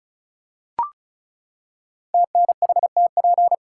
Morse code, heard as a keyed tone, about 36 wpm, 700 Hz. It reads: TNHTP
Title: Code Groups - 5 Letters - 35wpm